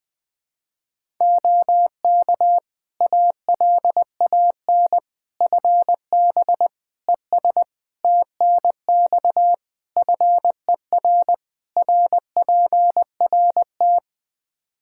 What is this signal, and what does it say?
Morse code, heard as a keyed tone, about 20 wpm, 700 Hz. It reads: OK ALAN FB ES TNX FER RPRT
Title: Practice QSO - Exchange 2 - Info - 20wpm